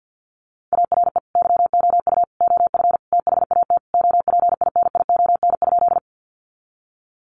Morse code, heard as a keyed tone, about 50 wpm, 700 Hz. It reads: ARE YOU OF THAT OPINION?
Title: Sentences from Top 1200 Words 50wpm